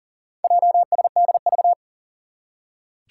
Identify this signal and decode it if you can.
Morse code, heard as a keyed tone, about 38 wpm, 700 Hz. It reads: JSDV
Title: Four Challenging Letter-Number Combinations 40wpm